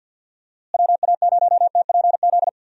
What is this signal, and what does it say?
Morse code, heard as a keyed tone, about 51 wpm, 700 Hz. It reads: WA0TPZ
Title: USA Call Signs - 6 Characters - Repeated 2x - 50wpm